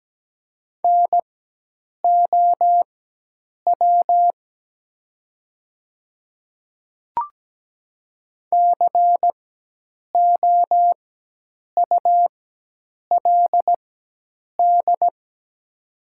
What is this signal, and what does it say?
Morse code, heard as a keyed tone, about 17 wpm, spaced as for 8 wpm, 700 Hz. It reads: NOW COULD
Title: 4x Character Spacing - Top 100 Words - 17wpm